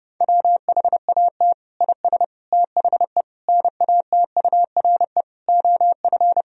Morse code, heard as a keyed tone, about 30 wpm, 700 Hz. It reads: WHAT IS THE NATURE OF
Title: Sentences from Top 1000 Words - Review Part 3 - 30wpm